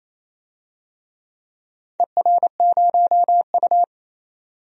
Morse code, heard as a keyed tone, about 29 wpm, 690 Hz. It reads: ER0U
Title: DXCC-100 Call Signs - 4 Characters - 28wpm